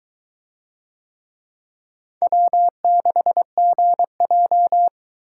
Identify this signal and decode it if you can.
Morse code, heard as a keyed tone, about 22 wpm, 700 Hz. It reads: W6GJ